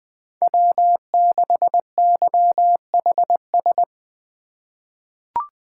Morse code, heard as a keyed tone, about 20 wpm, 700 Hz. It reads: W6YHS
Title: USA Call Signs - 5 Characters - 2x Character Spacing - 20wpm